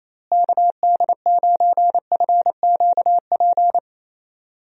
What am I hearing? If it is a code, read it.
Morse code, heard as a keyed tone, about 28 wpm, 700 Hz. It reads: KD9FQP
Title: USA Call Signs - 6 Characters - Repeated 2x - 28wpm